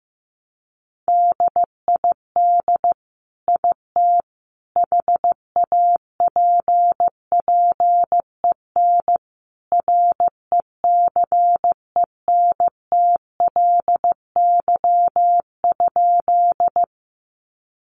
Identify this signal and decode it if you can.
Morse code, heard as a keyed tone, about 15 wpm, 700 Hz. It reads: DID IT HAPPEN RECENTLY?
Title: Sentences from Top 1200 Words 15wpm